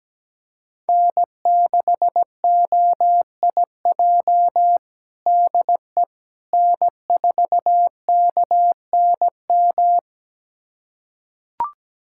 Morse code, heard as a keyed tone, about 17 wpm, 700 Hz. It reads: N6OIJ DE N4KNM
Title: USA Callsign DE Callsign - 5 Characters - 17wpm